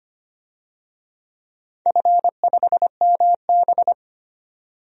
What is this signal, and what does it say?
Morse code, heard as a keyed tone, about 25 wpm, 700 Hz. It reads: F5MB
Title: DXCC-100 Call Signs - 4 Characters - 25wpm